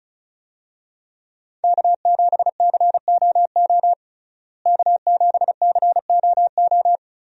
Morse code, heard as a keyed tone, about 34 wpm, 700 Hz. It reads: K7COO K7COO
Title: USA Call Signs - 5 Characters - Repeated 3x - 35wpm